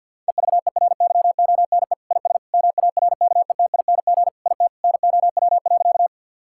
Morse code, heard as a keyed tone, about 50 wpm, 700 Hz. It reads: EVERYONE IS MARKETING IT NOW.
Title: Sentences from Top 1000 Words - Review Part 5 - 50wpm